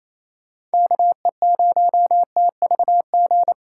Morse code, heard as a keyed tone, about 36 wpm, 700 Hz. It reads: KE0TVG